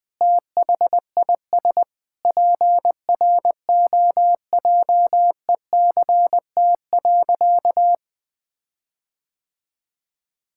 Morse code, heard as a keyed tone, about 21 wpm, 700 Hz. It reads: THIS PROJECT.
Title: Sentences with 6-Word Phrases - 20wpm